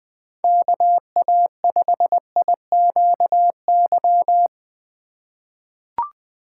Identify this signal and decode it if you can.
Morse code, heard as a keyed tone, about 20 wpm, 700 Hz. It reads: KA5IQY
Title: USA Call Signs - 2x Character Spacing - 20wpm